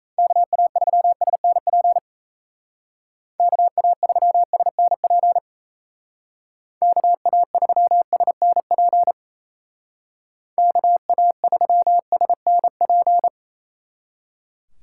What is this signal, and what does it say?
Morse code, heard as a keyed tone, about 34 wpm, 700 Hz. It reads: KA3SNP KA3SNP KA3SNP KA3SNP
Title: Speed Racing - USA 2x3 Call Signs 28wpm